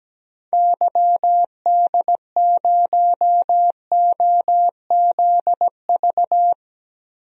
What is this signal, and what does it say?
Morse code, heard as a keyed tone, about 17 wpm, 700 Hz. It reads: YD0OZV
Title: DXCC-100 Call Signs - 6 Characters - Repeated 2x - 17wpm